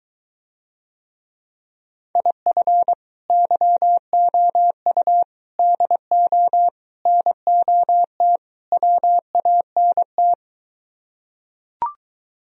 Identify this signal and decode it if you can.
Morse code, heard as a keyed tone, about 23 wpm, 700 Hz. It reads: IF YOU DO NOT WANT